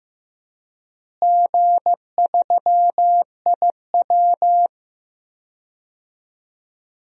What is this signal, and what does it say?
Morse code, heard as a keyed tone, about 15 wpm, 700 Hz. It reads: G3IW
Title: DXCC-100 Call Signs - 4 Characters - Repeated 3x - 15wpm